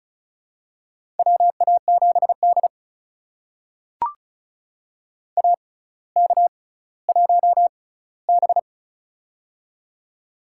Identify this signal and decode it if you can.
Morse code, heard as a keyed tone, about 36 wpm, spaced as for 18 wpm, 700 Hz. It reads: WA7D AK1B
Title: USA Call Signs - 4 Characters - 6x Character Spacing - 35wpm